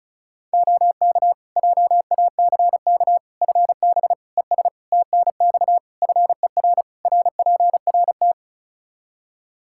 Morse code, heard as a keyed tone, about 36 wpm, 700 Hz. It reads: OK JACK FB ES TNX FER RPRT
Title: Practice QSO - Exchange 2 - Info - 35wpm